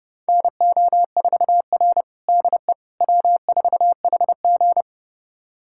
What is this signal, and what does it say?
Morse code, heard as a keyed tone, about 30 wpm, 700 Hz. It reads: NO4R DE W4HG
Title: USA Callsign DE Callsign - 4 Characters - 30wpm